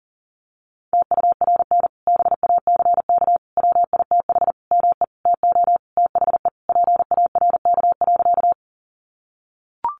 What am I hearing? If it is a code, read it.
Morse code, heard as a keyed tone, about 40 wpm, 700 Hz. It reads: TURN BACK WITH ME TO THE PARK.
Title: Sentences from Top 1300 Words 40wpm